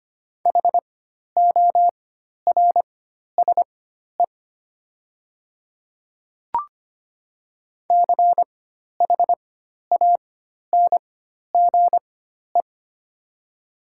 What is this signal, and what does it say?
Morse code, heard as a keyed tone, about 25 wpm, spaced as for 12 wpm, 700 Hz. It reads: HORSE CHANGE
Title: 4x Character Spacing - Top 500 Words - 25wpm